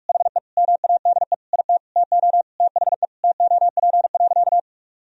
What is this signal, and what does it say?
Morse code, heard as a keyed tone, about 46 wpm, 700 Hz. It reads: HE MADE IT TO THE TOP.
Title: Sentences from Top 1000 Words - Review Part 2 - 45wpm